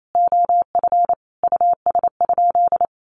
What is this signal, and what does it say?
Morse code, heard as a keyed tone, about 28 wpm, 700 Hz. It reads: OF US?
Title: Sentences from Top 1100 Words 28wpm